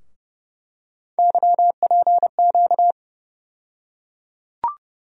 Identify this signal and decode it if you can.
Morse code, heard as a keyed tone, about 30 wpm, 700 Hz. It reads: YPQ
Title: USA Call Sign Postfixes - 3 Letters 30wpm